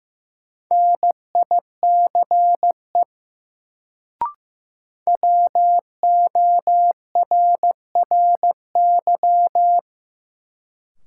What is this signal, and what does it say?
Morse code, heard as a keyed tone, about 16 wpm, 700 Hz. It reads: NICE WORRY
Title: Sentences from Top 1100 Words - New Word Drill Alt - 15wpm